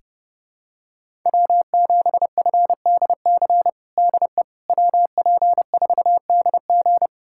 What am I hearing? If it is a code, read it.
Morse code, heard as a keyed tone, about 30 wpm, 700 Hz. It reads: W7FDC DE WP4DG
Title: USA Callsign DE Callsign - 5 Characters - 30wpm